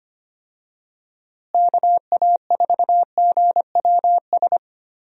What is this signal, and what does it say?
Morse code, heard as a keyed tone, about 25 wpm, 700 Hz. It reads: KA4GWS
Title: USA Call Signs - 6 Characters - Repeated 2x - 25wpm